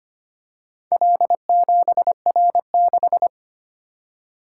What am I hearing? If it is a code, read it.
Morse code, heard as a keyed tone, about 25 wpm, 700 Hz. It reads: L7R6
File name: practice / 25wpm / Four Letter-Number Combinations - 25wpm